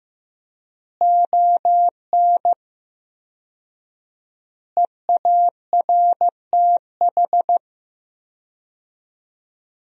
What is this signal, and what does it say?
Morse code, heard as a keyed tone, about 15 wpm, 700 Hz. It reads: ON EARTH
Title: 4x Word-Spacing - Sentences from Top 200 Words - 15wpm